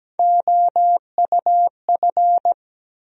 Morse code, heard as a keyed tone, about 17 wpm, 700 Hz. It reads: OUF
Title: USA Call Sign Postfixes - 3 Letters 17wpm